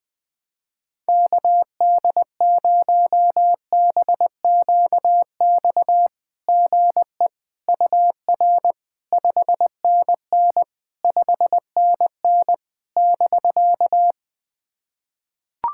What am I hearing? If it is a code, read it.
Morse code, heard as a keyed tone, about 20 wpm, 690 Hz. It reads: KD0BQX GE UR 5NN 5NN <BK>
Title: SOTA QSO Practice 20wpm